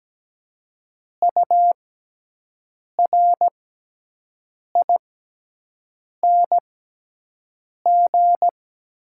Morse code, heard as a keyed tone, about 17 wpm, spaced as for 6 wpm, 700 Hz. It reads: URING
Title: 6x Character Spacing - Top 500 Words - 17wpm